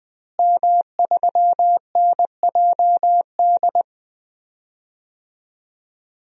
Morse code, heard as a keyed tone, about 21 wpm, 700 Hz. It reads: M3NJD
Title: DXCC-100 Call Signs - 5 Characters - Repeated 2x - 20wpm